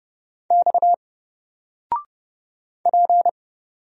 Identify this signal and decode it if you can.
Morse code, heard as a keyed tone, about 30 wpm, 700 Hz. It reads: X P
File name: Single-Letter-Number-30wpm